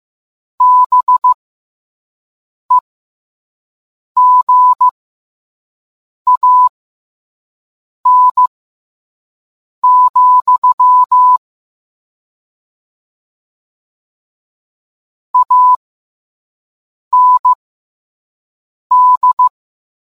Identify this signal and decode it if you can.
Morse code, heard as a keyed tone, about 15 wpm, spaced as for 5 wpm, 1000 Hz. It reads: BEGAN, AND